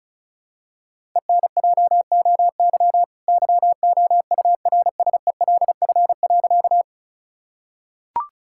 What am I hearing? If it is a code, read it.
Morse code, heard as a keyed tone, about 36 wpm, 700 Hz. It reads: ENJOY YOURSELF.
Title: Sentences from Top 1000 Words - Review Part 5 - 35wpm